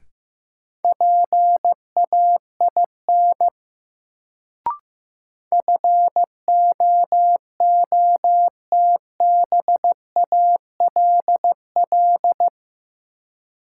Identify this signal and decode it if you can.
Morse code, heard as a keyed tone, about 15 wpm, 700 Hz. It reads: PAIN FOOTBALL
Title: Sentences from Top 1600 Words - New Word Drill Alt - 15wpm